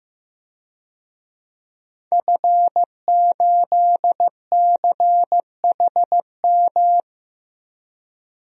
Morse code, heard as a keyed tone, about 15 wpm, 700 Hz. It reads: F8CHM
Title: DXCC-100 Call Signs - 5 Characters - 15wpm